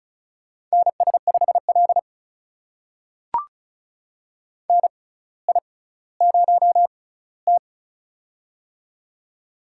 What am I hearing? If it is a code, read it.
Morse code, heard as a keyed tone, about 36 wpm, spaced as for 18 wpm, 700 Hz. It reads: NS5L NI0T
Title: USA Call Signs - 4 Characters - 6x Character Spacing - 35wpm